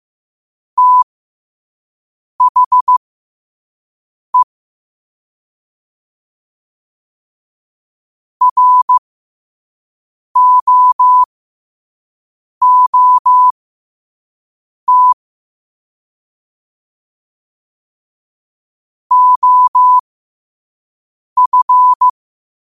Morse code, heard as a keyed tone, about 15 wpm, spaced as for 5 wpm, 1000 Hz. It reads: THE ROOT OF
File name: qotdfiles / 5wpm